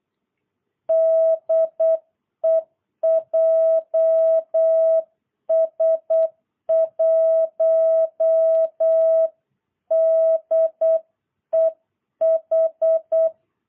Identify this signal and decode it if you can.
Morse code, heard as a keyed tone, about 8 wpm, 650 Hz. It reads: DE JS1DEH
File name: jh8cbh / 2022cw